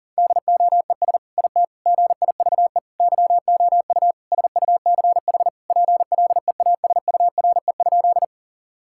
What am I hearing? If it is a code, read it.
Morse code, heard as a keyed tone, about 40 wpm, 700 Hz. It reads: DOES IT GIVE YOU SUCH PLEASURE?
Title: Sentences from Top 1600 Words 40wpm